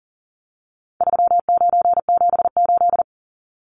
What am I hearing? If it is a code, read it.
Morse code, heard as a keyed tone, about 38 wpm, 700 Hz. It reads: 3978
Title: Numbers - 4-Digit 40wpm